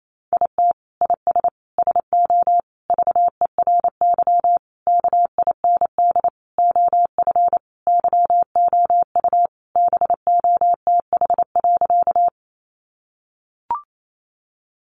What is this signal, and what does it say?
Morse code, heard as a keyed tone, about 28 wpm, 700 Hz. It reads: IT IS SO VERY KIND OF YOU BOTH.
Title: Sentences from Top 300 Words 28wpm